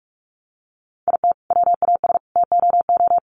Morse code, heard as a keyed tone, about 44 wpm, 700 Hz. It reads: IT WAS TOO
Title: Sentences from Top 1200 Words 45wpm